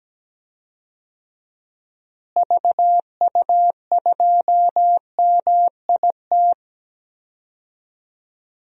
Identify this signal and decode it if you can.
Morse code, heard as a keyed tone, about 17 wpm, 700 Hz. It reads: VU2MIT
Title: DXCC-100 Call Signs - 6 Characters - Repeated 3x - 17wpm